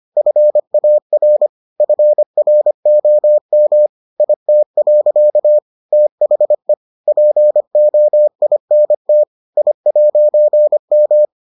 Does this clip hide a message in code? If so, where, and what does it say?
Yes: Morse code from 0.2 to 11.2 s, heard as a keyed tone, about 25 wpm, 600 Hz. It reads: FAR FROM IT. THE POINT I'M